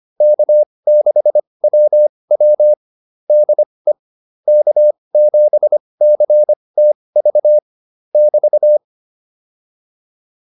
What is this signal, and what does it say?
Morse code, heard as a keyed tone, about 25 wpm, spaced as for 20 wpm, 600 Hz. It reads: K6WW DE K7CTV <BT>